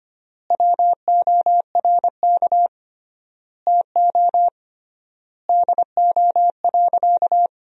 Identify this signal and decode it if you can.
Morse code, heard as a keyed tone, about 25 wpm, 700 Hz. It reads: WORK TO DO.
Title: 3x Word-Spacing - Sentences from Top 200 Words - 25wpm